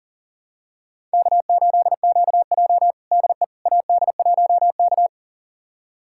Morse code, heard as a keyed tone, about 40 wpm, 700 Hz. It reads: K8QJ DE AD1K
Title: USA Callsign DE Callsign - 4 Characters - 40wpm